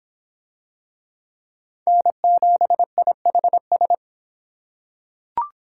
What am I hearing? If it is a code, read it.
Morse code, heard as a keyed tone, about 24 wpm, 700 Hz. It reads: N7IHS